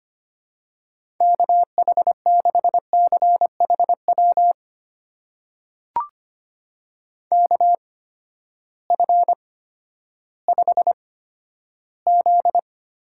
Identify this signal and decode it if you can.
Morse code, heard as a keyed tone, about 25 wpm, spaced as for 13 wpm, 700 Hz. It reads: KH6CHW KF5Z
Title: USA Call Signs - 6 Characters - 8x Character Spacing - 25wpm